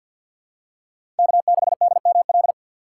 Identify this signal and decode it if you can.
Morse code, heard as a keyed tone, about 48 wpm, 700 Hz. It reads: K6DML
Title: USA Call Signs - 5 Characters - 3x Character Spacing - 50wpm